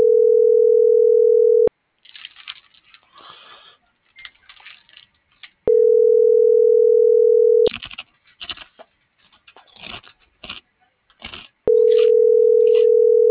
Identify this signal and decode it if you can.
Ringback tone (North America), 440 + 480 Hz.